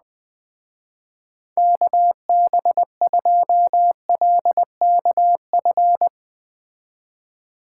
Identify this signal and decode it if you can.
Morse code, heard as a keyed tone, about 20 wpm, 700 Hz. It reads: KB2LKF